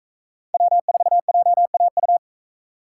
Morse code, heard as a keyed tone, about 41 wpm, 700 Hz. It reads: W4JAU